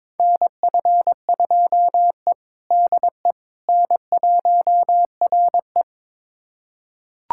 Morse code, heard as a keyed tone, about 21 wpm, 700 Hz. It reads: NF2E DE N1RE
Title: USA Callsign DE Callsign - 4 Characters - 22wpm